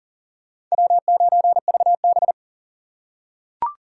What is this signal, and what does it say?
Morse code, heard as a keyed tone, about 40 wpm, 700 Hz. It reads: W9VB
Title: USA Call Signs - 4 Characters - 6x Character Spacing - 40wpm